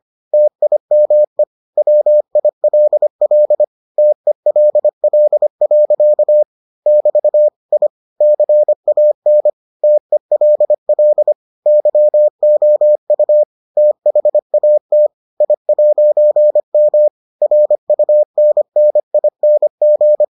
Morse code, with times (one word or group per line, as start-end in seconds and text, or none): 0.3-1.4 TIME
1.8-3.6 WILL
4.0-6.4 TELL.
6.9-7.5 <BT>
7.7-7.9 I
8.2-9.5 CAN
9.8-11.3 TELL
11.7-13.4 YOU
13.8-15.1 THAT
15.4-17.1 I'M
17.4-20.2 RUNNING